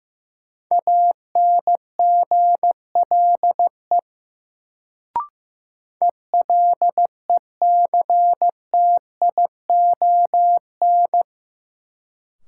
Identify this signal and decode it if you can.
Morse code, heard as a keyed tone, about 15 wpm, 700 Hz. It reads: ANGLE ELECTION
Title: Sentences from Top 1500 Words - New Word Drill - 15wpm